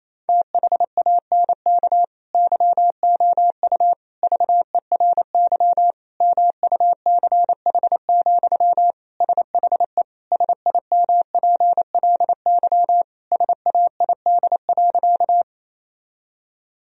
Morse code, heard as a keyed tone, about 28 wpm, 700 Hz. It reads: THANK YOU VERY MUCH, SHE SIMPLY SAID.
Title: Sentences from Top 1000 Words - Review Part 4 - 28wpm